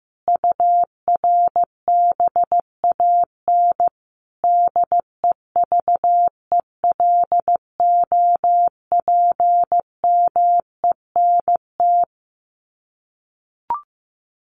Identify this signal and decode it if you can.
Morse code, heard as a keyed tone, about 15 wpm, 700 Hz. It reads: URBAN DEVELOPMENT
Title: Sets of 5 Words Encore2 15wpm